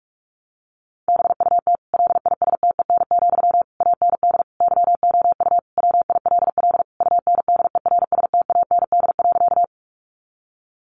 Morse code, heard as a keyed tone, about 44 wpm, 700 Hz. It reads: BUT LISTEN, AND YOU WILL UNDERSTAND.
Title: Sentences from Top 600 Words 45wpm